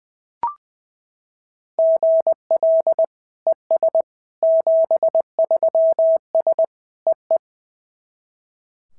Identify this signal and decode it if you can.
Morse code, heard as a keyed tone, about 20 wpm, 650 Hz. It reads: GL ES 73S EE
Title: SOTA QSO Practice 20wpm